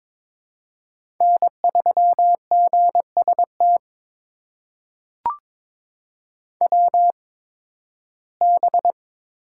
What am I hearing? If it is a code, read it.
Morse code, heard as a keyed tone, about 21 wpm, spaced as for 14 wpm, 700 Hz. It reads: N3GST WB
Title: USA Call Signs - 5 Characters - 8x Character Spacing - 22wpm